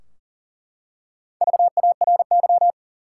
Morse code, heard as a keyed tone, about 40 wpm, 700 Hz. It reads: VARY
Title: Four Letter Words - 40wpm